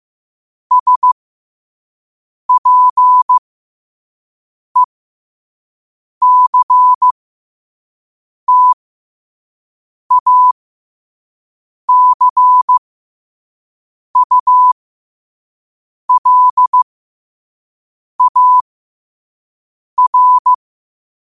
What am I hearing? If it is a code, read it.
Morse code, heard as a keyed tone, about 15 wpm, spaced as for 5 wpm, 1000 Hz. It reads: SPECTACULAR
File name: qotdfiles / 5wpm